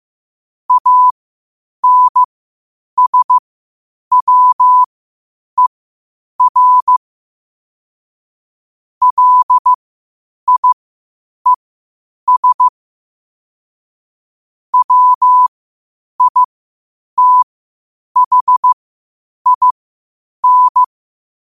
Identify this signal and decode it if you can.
Morse code, heard as a keyed tone, about 15 wpm, spaced as for 8 wpm, 1000 Hz. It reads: ANSWER LIES WITHIN
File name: qotdfiles / 8wpm